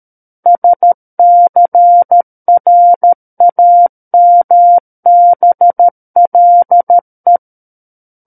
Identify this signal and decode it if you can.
Morse code, heard as a keyed tone, about 13 wpm, 700 Hz. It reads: SCRAMBLE
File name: Book One / 04-Chapter 4 - 13 WPM